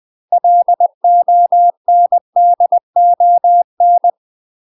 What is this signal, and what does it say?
Morse code, heard as a keyed tone, about 20 wpm, 700 Hz. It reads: LONDON
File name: UK MASTER QSOs TRAINING/20WPM